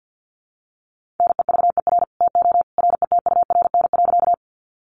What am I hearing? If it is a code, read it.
Morse code, heard as a keyed tone, about 50 wpm, 700 Hz. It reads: NEVER TO RETURN.
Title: Sentences from Top 800 Words 50wpm